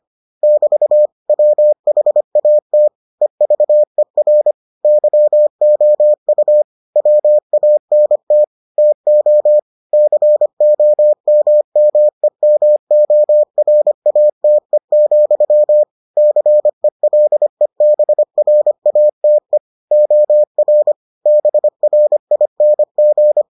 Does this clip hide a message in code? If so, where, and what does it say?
Yes: Morse code from 0.4 to 23.4 s, heard as a keyed tone, about 25 wpm, 600 Hz. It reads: <BT> WHAT EVER YOU WANT TO COMMEMORATE, CELEBRATE OR BRING